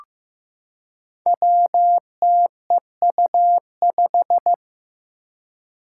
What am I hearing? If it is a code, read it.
Morse code, heard as a keyed tone, about 15 wpm, 700 Hz. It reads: WTEU5
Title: Five Challenging Letter-Number Combinations 15wpm